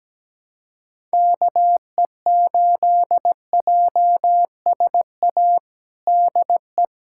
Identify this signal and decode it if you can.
Morse code, heard as a keyed tone, about 17 wpm, 700 Hz. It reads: KE8JSA DE
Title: USA Callsign DE Callsign - 6 Characters - 17wpm